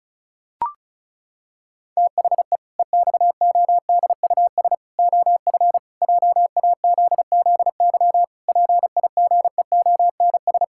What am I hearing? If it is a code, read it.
Morse code, heard as a keyed tone, about 36 wpm, 700 Hz. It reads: THE EXODUS OF JAZZY PIGEONS
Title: Pangram Challenge v2 - 35wpm